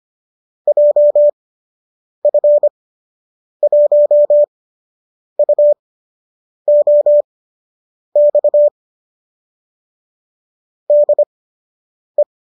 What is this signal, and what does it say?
Morse code, heard as a keyed tone, about 25 wpm, spaced as for 8 wpm, 590 Hz. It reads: JF1UOX DE